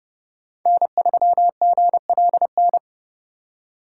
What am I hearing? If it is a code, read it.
Morse code, heard as a keyed tone, about 29 wpm, 700 Hz. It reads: N3GLN